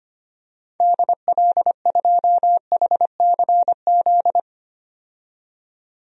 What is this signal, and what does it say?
Morse code, heard as a keyed tone, about 25 wpm, 700 Hz. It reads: DL2HCZ
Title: DXCC-100 Call Signs - 6 Characters - 25wpm